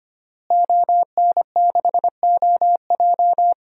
Morse code, heard as a keyed tone, about 25 wpm, 700 Hz. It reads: ON6OJ
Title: DXCC-100 Call Signs - 5 Characters - Repeated 3x - 25wpm